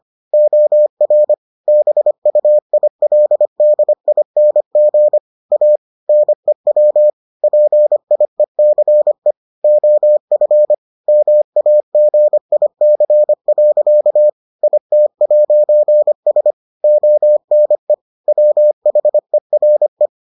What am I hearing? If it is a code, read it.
Morse code, heard as a keyed tone, about 25 wpm, 600 Hz. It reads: OR BUILDING A NEW PIECE OF MAGIC. IT'S ONE WHERE